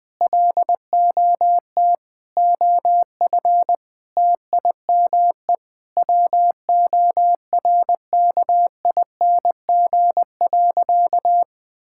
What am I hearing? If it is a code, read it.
Morse code, heard as a keyed tone, about 20 wpm, 700 Hz. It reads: LOT OF TIME WORKING.
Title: Sentences with 5-Word Phrases - Spoken Beginning - 20wpm